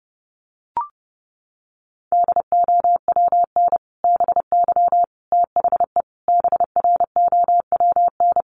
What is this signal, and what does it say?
Morse code, heard as a keyed tone, about 30 wpm, 700 Hz. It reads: DOWN BY THE BROWN
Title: Sentences from Top 1200 Words 30wpm